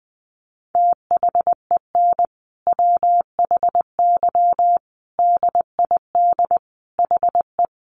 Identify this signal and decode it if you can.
Morse code, heard as a keyed tone, about 20 wpm, 700 Hz. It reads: THEN WHY DID HE
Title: Sentences from Top 800 Words 20wpm